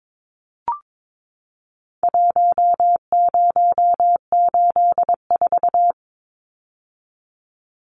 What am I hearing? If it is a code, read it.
Morse code, heard as a keyed tone, about 22 wpm, 700 Hz. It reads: 1084
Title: Numbers - Contest Style - 4-Digit 22wpm